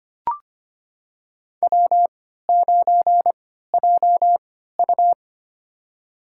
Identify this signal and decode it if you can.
Morse code, heard as a keyed tone, about 25 wpm, spaced as for 14 wpm, 700 Hz. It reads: W9JU
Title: USA Call Signs - 4 Characters - 3x Character Spacing - 25wpm